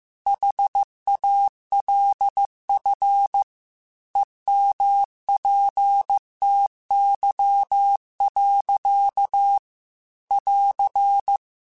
Morse code, heard as a keyed tone, about 15 wpm, 800 Hz. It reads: HALF EMPTY. <AR>
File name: Chuckles / Chuckles-mp3s / Glass Half Full @13